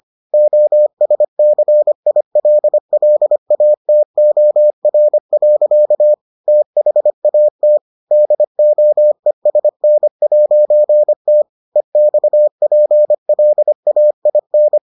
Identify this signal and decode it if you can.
Morse code, heard as a keyed tone, about 25 wpm, 600 Hz. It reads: OSCILLATOR. THAT DOESN'T EXPLAIN